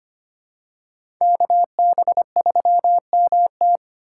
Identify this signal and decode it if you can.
Morse code, heard as a keyed tone, about 25 wpm, 700 Hz. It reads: KB3MT
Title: USA Call Signs - 5 Characters - 3x Character Spacing - 25wpm